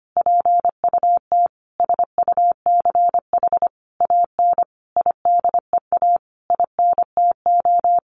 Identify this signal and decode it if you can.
Morse code, heard as a keyed tone, about 25 wpm, 700 Hz. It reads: PUT SUCH AN IDEA INTO
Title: Sentences from Top 600 Words 25wpm